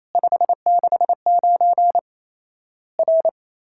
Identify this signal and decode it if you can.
Morse code, heard as a keyed tone, about 28 wpm, 700 Hz. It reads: 569 R
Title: Worked All Europe DX Contest - QTC Exchange - 28wpm